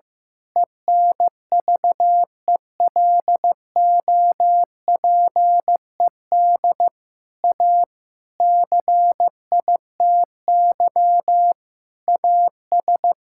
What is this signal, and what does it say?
Morse code, heard as a keyed tone, about 16 wpm, 700 Hz. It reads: ENVELOPED A CITY AS
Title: Pangram Challenge v2 - 15wpm